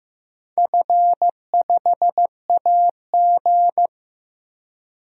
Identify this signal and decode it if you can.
Morse code, heard as a keyed tone, about 16 wpm, 700 Hz. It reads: F5AG
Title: DXCC-100 Call Signs - 4 Characters - Repeated 3x - 15wpm